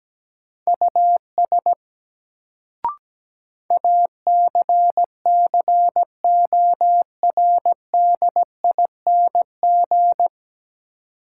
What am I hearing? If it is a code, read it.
Morse code, heard as a keyed tone, about 17 wpm, 700 Hz. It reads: US ACCORDING
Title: Sets of 2 Words - New Word Drill - 17wpm